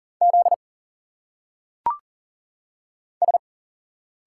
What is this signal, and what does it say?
Morse code, heard as a keyed tone, about 40 wpm, 700 Hz. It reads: Z S
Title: Single Letters - Phonetic Pronunciation 40wpm